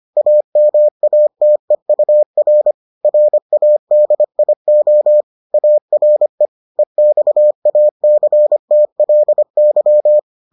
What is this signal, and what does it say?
Morse code, heard as a keyed tone, about 25 wpm, 600 Hz. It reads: AMATEUR RADIO ARE EXACTLY